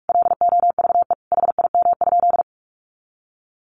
Morse code, heard as a keyed tone, about 46 wpm, 700 Hz. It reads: LOVE HIM?
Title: Sentences from Top 400 Words 45wpm